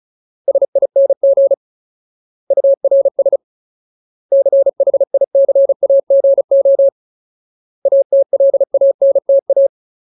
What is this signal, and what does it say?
Morse code, heard as a keyed tone, about 34 wpm, 550 Hz. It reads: SING URS CHICAGO ATLANTA
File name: Ultimate Word List / 35WPM Ultimate